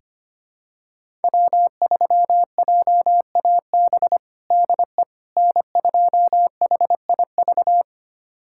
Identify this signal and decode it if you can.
Morse code, heard as a keyed tone, about 25 wpm, 700 Hz. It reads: W3JAB DE N2HIV
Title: USA Callsign DE Callsign - 5 Characters - 25wpm